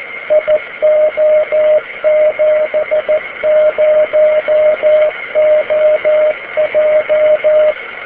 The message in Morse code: IO70OJ